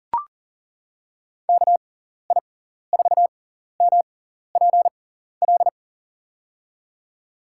Morse code, heard as a keyed tone, about 40 wpm, spaced as for 14 wpm, 690 Hz. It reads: KI4MPL
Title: USA Call Signs - 6x Character Spacing - 40wpm